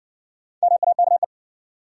Morse code, heard as a keyed tone, about 60 wpm, 700 Hz. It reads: MAKE